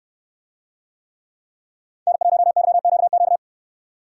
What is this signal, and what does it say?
Morse code, heard as a keyed tone, about 66 wpm, 700 Hz. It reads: N1YQQ